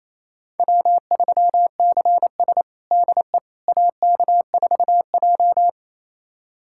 Morse code, heard as a keyed tone, about 28 wpm, 700 Hz. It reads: W3CS DE AK4J
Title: USA Callsign DE Callsign - 4 Characters - 28wpm